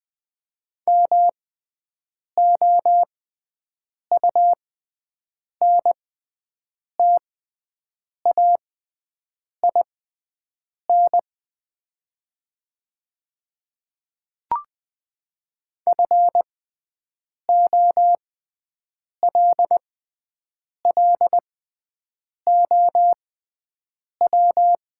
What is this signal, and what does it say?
Morse code, heard as a keyed tone, about 21 wpm, spaced as for 7 wpm, 700 Hz. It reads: MOUNTAIN FOLLOW
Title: 6x Character Spacing - Top 500 Words - 20wpm